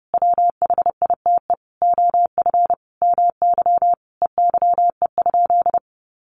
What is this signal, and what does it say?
Morse code, heard as a keyed tone, about 28 wpm, 700 Hz. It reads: WHITE OF MY EYE?
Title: Sentences from Top 500 Words 30wpm